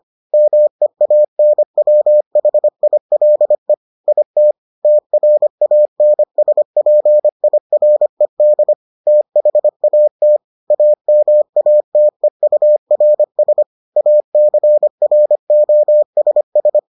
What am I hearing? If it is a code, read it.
Morse code, heard as a keyed tone, about 25 wpm, 600 Hz. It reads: MEANWHILE IT TRANSPIRED THAT AMATEURS ACROSS